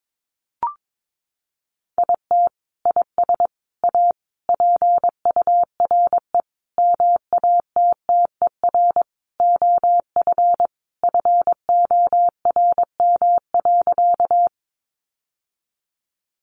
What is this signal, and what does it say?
Morse code, heard as a keyed tone, about 22 wpm, 700 Hz. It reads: IT IS A PURE MATTER OF FORM.
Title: Sentences from Top 1500 Words 22wpm